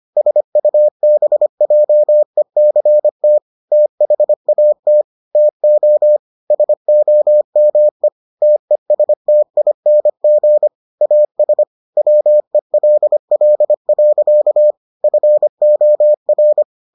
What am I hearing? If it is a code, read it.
Morse code, heard as a keyed tone, about 25 wpm, 600 Hz. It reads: SUBJECT THAT TO SOME TESTING AS WELL. FOR